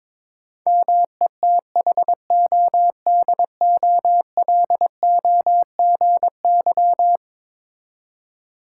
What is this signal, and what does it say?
Morse code, heard as a keyed tone, about 22 wpm, 700 Hz. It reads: METHODOLOGY
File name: practice / 22wpm / Eleven Letter Words - 22wpm